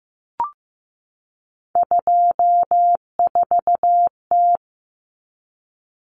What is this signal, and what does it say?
Morse code, heard as a keyed tone, about 15 wpm, 700 Hz. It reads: 24T
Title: Common Cut Numbers - 3-Digit 15wpm